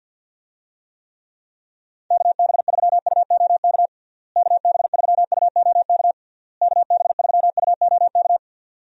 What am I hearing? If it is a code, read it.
Morse code, heard as a keyed tone, about 50 wpm, 700 Hz. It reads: KB3UOK KB3UOK KB3UOK
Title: USA Call Signs - 6 Characters - Repeated 3x - 50wpm